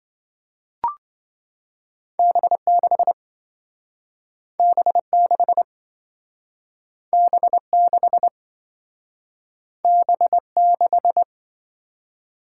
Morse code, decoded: B6 B6 B6 B6

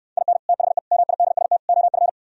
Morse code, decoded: IN THE MIDST OF

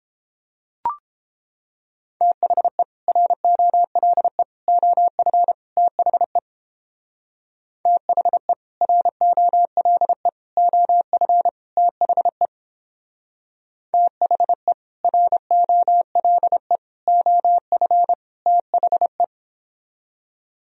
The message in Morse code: THE ROLE OF THE THE ROLE OF THE THE ROLE OF THE

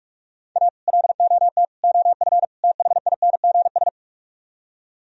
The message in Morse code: A LOT OF THINGS

